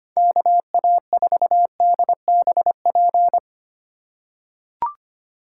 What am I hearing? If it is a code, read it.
Morse code, heard as a keyed tone, about 25 wpm, 700 Hz. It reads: KA4DBP
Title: USA Call Signs - 6 Characters - 4x Character Spacing - 25wpm